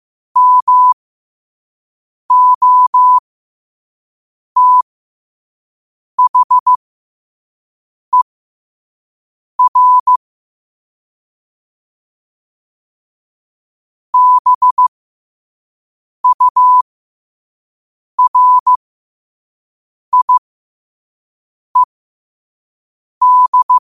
Morse code, heard as a keyed tone, about 15 wpm, spaced as for 5 wpm, 1000 Hz. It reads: MOTHER BURIED
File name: 5wpm